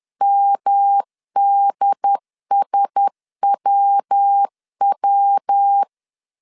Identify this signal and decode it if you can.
Morse code, heard as a keyed tone, about 10 wpm, 800 Hz. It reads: MDSWW
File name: CW / DET / 1 9 1500C S B 10wpm 4439